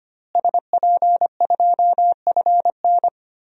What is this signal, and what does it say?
Morse code, heard as a keyed tone, about 25 wpm, 700 Hz. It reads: SP2FN
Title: DXCC-100 Call Signs - 5 Characters - Repeated 2x - 25wpm